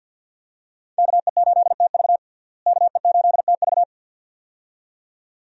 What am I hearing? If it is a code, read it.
Morse code, heard as a keyed tone, about 52 wpm, 700 Hz. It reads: KE8TV KE8TV
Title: USA Call Signs - 5 Characters - Repeated 2x - 50wpm